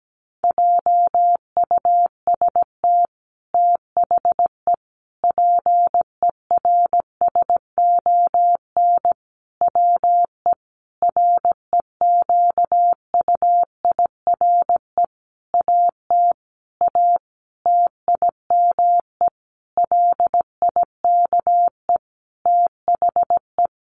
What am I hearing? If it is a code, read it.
Morse code, heard as a keyed tone, about 17 wpm, 700 Hz. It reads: JUST THE PERSON WE REQUIRE AT A TIME LIKE THE
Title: Sentences from Top 900 Words 17wpm